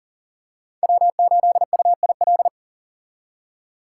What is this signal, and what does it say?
Morse code, heard as a keyed tone, about 40 wpm, 700 Hz. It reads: W8UIL
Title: K1USN SST QSO Practice - 40wpm